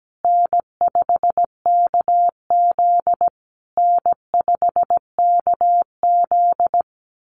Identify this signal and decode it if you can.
Morse code, heard as a keyed tone, about 17 wpm, 700 Hz. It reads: N5KZ N5KZ
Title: USA Call Signs Repeated 3x 17wpm